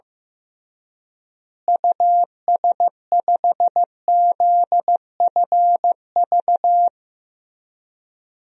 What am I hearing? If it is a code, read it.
Morse code, heard as a keyed tone, about 15 wpm, 700 Hz. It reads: US5ZFV
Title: DXCC-100 Call Signs - 6 Characters - Repeated 3x - 15wpm